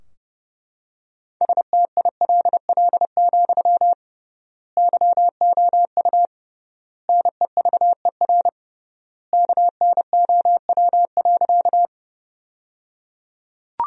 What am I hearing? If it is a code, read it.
Morse code, heard as a keyed tone, about 30 wpm, 700 Hz. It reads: STILL, YOU NEVER KNOW.
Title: 3x Word-Spacing - Sentences from Top 300 Words - 30wpm